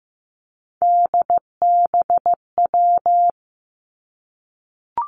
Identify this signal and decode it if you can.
Morse code, heard as a keyed tone, about 15 wpm, 700 Hz. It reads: DBW
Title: Three Challenging Letter-Number Combinations 15wpm